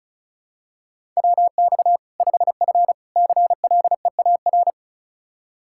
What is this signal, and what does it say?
Morse code, heard as a keyed tone, about 36 wpm, 700 Hz. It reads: WX 5F CLEAR
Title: Practice QSO - Exchange 2 - Info - 35wpm